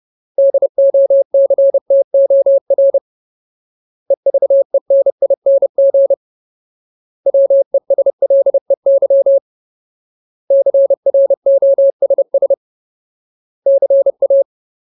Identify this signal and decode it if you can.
Morse code, heard as a keyed tone, about 30 wpm, 550 Hz. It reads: DOCTOR EVENING WESLEY CROSS CA
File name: Ultimate Word List / 30WPM Ultimate